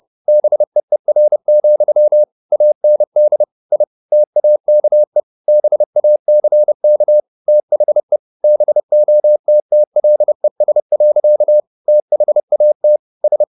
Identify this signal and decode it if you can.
Morse code, heard as a keyed tone, about 30 wpm, 600 Hz. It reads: BEER, AND I TAKE BACK THE BOTTLES. THAT S
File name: the old man and the sea / the old man and the sea 1 30wpm